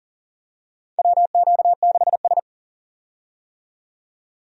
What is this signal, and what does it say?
Morse code, heard as a keyed tone, about 37 wpm, 700 Hz. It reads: WQ6S
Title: USA Call Signs - 4 Characters - 40wpm